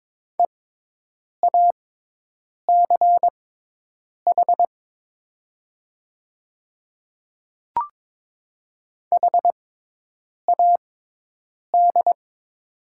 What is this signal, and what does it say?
Morse code, heard as a keyed tone, about 22 wpm, spaced as for 8 wpm, 700 Hz. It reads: EACH HAD
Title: 6x Character Spacing - Top 100 Words - 22wpm